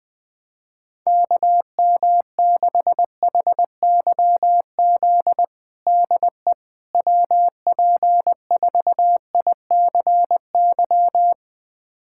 Morse code, heard as a keyed tone, about 20 wpm, 700 Hz. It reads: KM6HYZ DE WP4ICY
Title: USA Callsign DE Callsign - 6 Characters - 20wpm